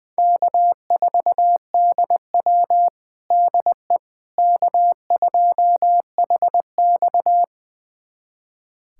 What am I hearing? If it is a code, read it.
Morse code, heard as a keyed tone, about 20 wpm, 700 Hz. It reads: K4DW DE K2HX
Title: USA Callsign DE Callsign - 4 Characters - 20wpm